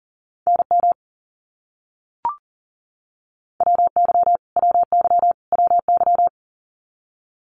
Morse code, heard as a keyed tone, about 40 wpm, 700 Hz. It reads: NM WY WY WY
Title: US States Repeated 3x 40wpm